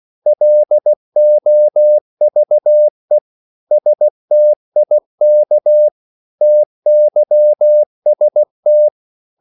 Morse code, heard as a keyed tone, about 16 wpm, 600 Hz. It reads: LOVE STIK TYST
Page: Korte danske ord 16wpm